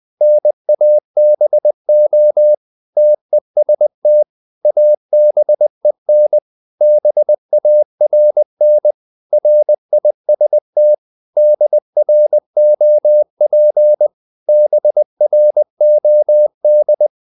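Morse code, heard as a keyed tone, about 20 wpm, 600 Hz. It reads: NABO TEST ABEN BARN RIST DROP BROD
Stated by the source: Morse code practice of short Danish words at 20 wpm, from CW med Gnister.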